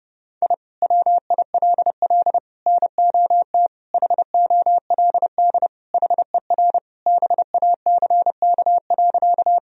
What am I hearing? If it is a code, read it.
Morse code, heard as a keyed tone, about 32 wpm, 700 Hz. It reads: I WILL NOT HOLD HER BACK.
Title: Sentences from Top 1000 Words - Review Part 2 - 30wpm